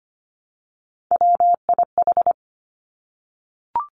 Morse code, heard as a keyed tone, about 25 wpm, 700 Hz. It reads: WIH
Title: Three Challenging Letter-Number Combinations 25wpm